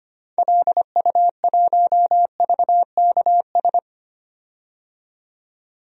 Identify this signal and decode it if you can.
Morse code, heard as a keyed tone, about 25 wpm, 700 Hz. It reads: LU1VKS
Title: DXCC-100 Call Signs - 6 Characters - 25wpm